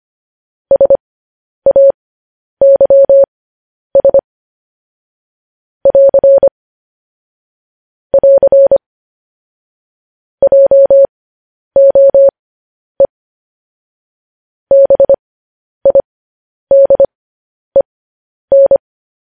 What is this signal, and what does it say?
Morse code, heard as a keyed tone, about 25 wpm, spaced as for 10 wpm, 550 Hz. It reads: SAYS <AR> <AR> JOE BIDEN
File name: podcast / News headlines-1x-25x10